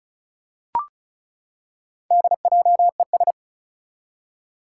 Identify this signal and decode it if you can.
Morse code, heard as a keyed tone, about 34 wpm, 700 Hz. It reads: DJES
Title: Four Challenging Letter-Number Combinations 35wpm